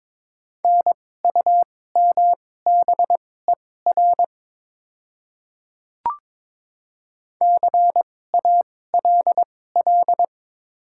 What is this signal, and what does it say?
Morse code, heard as a keyed tone, about 22 wpm, spaced as for 16 wpm, 700 Hz. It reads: NUMBER CALL
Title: 2x Character Spacing - Top 100 Words - 22wpm